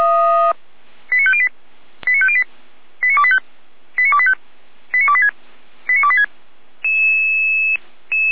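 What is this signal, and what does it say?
Five-tone selective call (ZVEI), heard as quick runs of five tones, tones of about 70 ms. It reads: 89508 89508 89285 89285 89286 89286
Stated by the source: ZVEI five-tone selective call for address 89220, received by SDR at feuerwehr-orlamuende (Orlamünde fire brigade).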